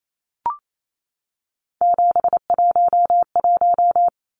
Morse code, heard as a keyed tone, about 28 wpm, 700 Hz. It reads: 711
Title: Numbers - 3-Digit 28wpm